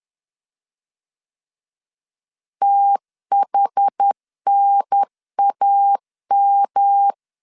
Morse code, heard as a keyed tone, about 10 wpm, 800 Hz. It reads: THNAM
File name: CW / DET / 1 3 1500C S C 10wpm 3335